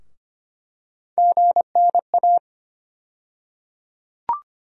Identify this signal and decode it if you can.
Morse code, heard as a keyed tone, about 25 wpm, 700 Hz. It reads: GNA
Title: USA Call Sign Postfixes - 3 Letters 25wpm